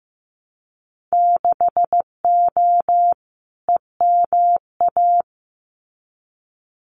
Morse code, heard as a keyed tone, about 15 wpm, 700 Hz. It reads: 6O EMA
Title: Winter Field Day Exchanges Encore 15wpm